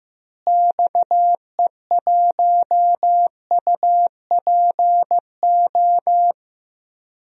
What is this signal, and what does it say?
Morse code, heard as a keyed tone, about 15 wpm, 700 Hz. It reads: XE1UPO
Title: DXCC-100 Call Signs - 6 Characters - Repeated 3x - 15wpm